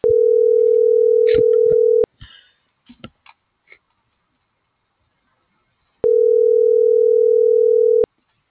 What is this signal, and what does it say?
Ringback tone (North America), 440 + 480 Hz.